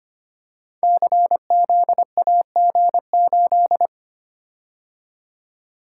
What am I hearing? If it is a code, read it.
Morse code, heard as a keyed tone, about 25 wpm, 700 Hz. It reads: CZAG8
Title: Code Groups - 5 Letter:Numbers - 25wpm